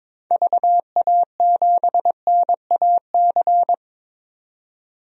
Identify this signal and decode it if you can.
Morse code, heard as a keyed tone, about 22 wpm, 700 Hz. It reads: VA7NAC
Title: DXCC-100 Call Signs - 6 Characters - 22wpm